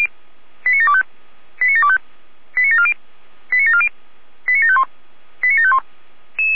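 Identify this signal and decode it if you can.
Five-tone selective call (ZVEI), heard as quick runs of five tones, tones of about 70 ms. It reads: 89725 89725 89740 89740 89751 89751
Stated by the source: ZVEI five-tone selective call received on SDR.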